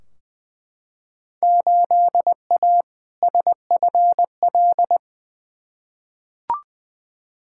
Morse code, heard as a keyed tone, about 20 wpm, 700 Hz. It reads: 8A SFL
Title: Field Day Exchanges Encore - 20wpm